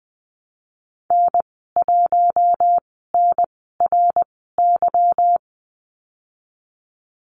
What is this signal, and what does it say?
Morse code, heard as a keyed tone, about 20 wpm, spaced as for 15 wpm, 700 Hz. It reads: N1NRY
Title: 173 Lesson 28g - Number 9 - Call Signs 20wpm